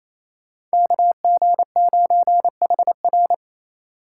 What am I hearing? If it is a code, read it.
Morse code, heard as a keyed tone, about 33 wpm, 700 Hz. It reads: KG9HR
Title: Speed Racing - USA 2x2 Call Signs 28wpm